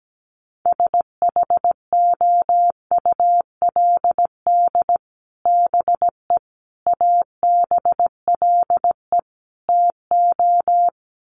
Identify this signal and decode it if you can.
Morse code, heard as a keyed tone, about 17 wpm, 700 Hz. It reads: SHOULD BE ABLE TO
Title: Sets of 4 Words 17wpm